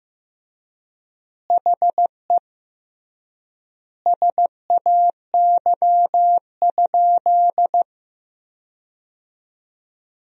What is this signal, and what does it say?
Morse code, heard as a keyed tone, about 15 wpm, 700 Hz. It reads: HE SAY?
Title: 3x Word-Spacing - Sentences from Top 200 Words - 15wpm